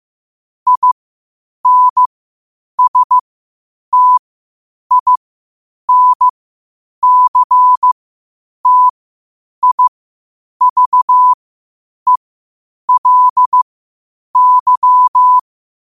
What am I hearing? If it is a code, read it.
Morse code, heard as a keyed tone, about 15 wpm, spaced as for 8 wpm, 1000 Hz. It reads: INSTINCTIVELY
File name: qotdfiles / 8wpm